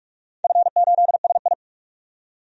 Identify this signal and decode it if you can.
Morse code, heard as a keyed tone, about 42 wpm, 700 Hz. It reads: F8SI
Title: DXCC-100 Call Signs - 4 Characters - 45wpm